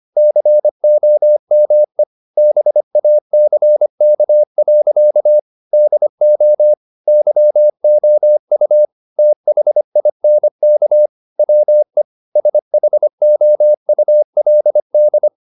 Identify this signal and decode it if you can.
Morse code, heard as a keyed tone, about 25 wpm, 600 Hz. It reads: COME BACK. DO YOU THINK WE SHOULD
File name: the old man and the sea part 1 25wpm